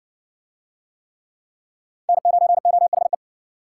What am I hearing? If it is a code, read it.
Morse code, heard as a keyed tone, about 60 wpm, 700 Hz. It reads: N9OHE